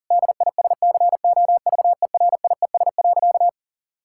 Morse code, heard as a keyed tone, about 40 wpm, 700 Hz. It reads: DISCOVERIES.
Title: Pangram Challenge v2 - 40wpm